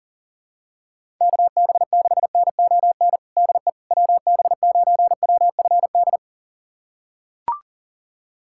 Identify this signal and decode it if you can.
Morse code, heard as a keyed tone, about 40 wpm, 700 Hz. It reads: KB6NON DE WB9WFD
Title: USA Callsign DE Callsign 40wpm